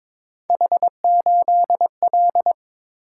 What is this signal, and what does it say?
Morse code, heard as a keyed tone, about 26 wpm, 700 Hz. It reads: H8L
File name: Speed-Racing - Three Letter-Number Combinations - 22wpm